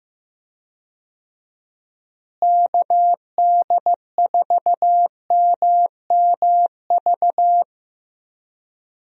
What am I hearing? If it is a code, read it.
Morse code, heard as a keyed tone, about 15 wpm, 700 Hz. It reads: KD4MMV
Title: USA Call Signs - 6 Characters - Repeated 3x - 15wpm